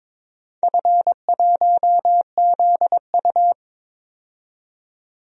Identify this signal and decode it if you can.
Morse code, heard as a keyed tone, about 22 wpm, 700 Hz. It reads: F1ZU
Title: DXCC-100 Call Signs - 4 Characters - 22wpm